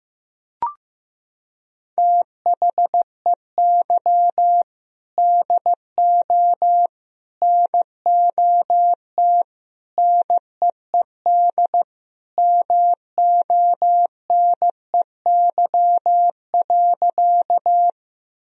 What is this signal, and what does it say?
Morse code, heard as a keyed tone, about 15 wpm, 700 Hz. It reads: THEY DO NOT NEED MONEY.
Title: Sentences from Top 1000 Words - Review Part 1 - 15wpm